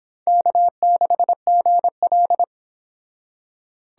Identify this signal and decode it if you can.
Morse code, heard as a keyed tone, about 25 wpm, 700 Hz. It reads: K6GL